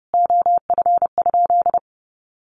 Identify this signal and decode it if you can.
Morse code, heard as a keyed tone, about 30 wpm, 700 Hz. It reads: OF?
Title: Sentences from Top 1400 Words 30wpm